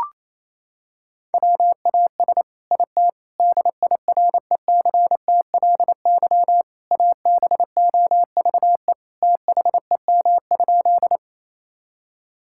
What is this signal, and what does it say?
Morse code, heard as a keyed tone, about 28 wpm, 700 Hz. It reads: WAS IT DIRECTLY ABOVE THEM?
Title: Sentences from Top 1000 Words - Review Part 5 - 28wpm